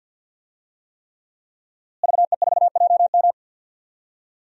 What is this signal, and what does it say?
Morse code, heard as a keyed tone, about 53 wpm, 700 Hz. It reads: VE4JM
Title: DXCC-100 Call Signs - 5 Characters - Repeated 3x - 50wpm